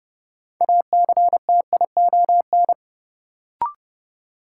Morse code, heard as a keyed tone, about 30 wpm, 700 Hz. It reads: ACTION
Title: Sentences from Top 700 Words - New Word Drill - 30wpm